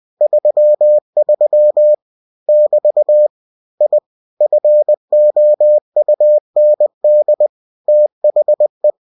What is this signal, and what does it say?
Morse code, heard as a keyed tone, about 20 wpm, 600 Hz. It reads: 33 <BT> I FOUND THE
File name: cw-books / winnie-the-pooh / winnie-20wpm-page-0033